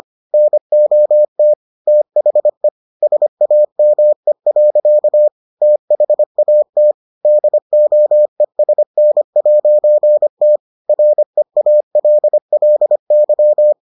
Morse code, heard as a keyed tone, about 25 wpm, 600 Hz. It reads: NOT THE SAME. THAT DOESN'T REALLY